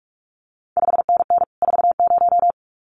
Morse code, heard as a keyed tone, about 46 wpm, 700 Hz. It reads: 5NN 40
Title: CQ WW DX Contest Exchange - 45wpm